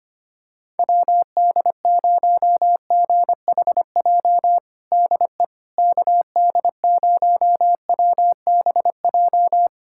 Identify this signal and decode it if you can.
Morse code, heard as a keyed tone, about 25 wpm, 700 Hz. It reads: WD0GHJ DE KD0WBJ
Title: USA Callsign DE Callsign - 6 Characters - 25wpm